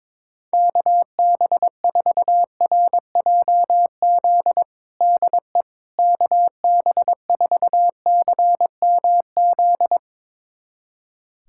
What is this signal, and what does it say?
Morse code, heard as a keyed tone, about 22 wpm, 700 Hz. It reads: KB4RJZ DE KB4CMZ
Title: USA Callsign DE Callsign - 6 Characters - 22wpm